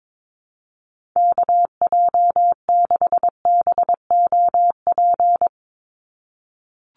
Morse code, heard as a keyed tone, about 22 wpm, 700 Hz. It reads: KJ6BOP
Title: USA Call Signs 22wpm